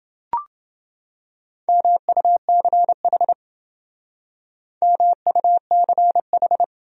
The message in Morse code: MUCH MUCH

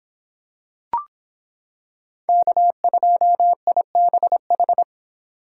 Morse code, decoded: K2IBH

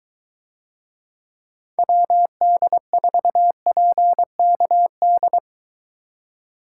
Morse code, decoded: WD4PKD

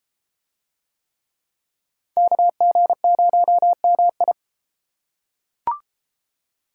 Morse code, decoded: KG0MI

